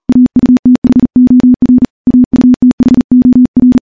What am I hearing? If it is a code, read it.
Morse code, heard as a keyed tone, about 40 wpm, 250 Hz. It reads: AUTHOR AUTHOR
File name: morse / morse-training-40WPM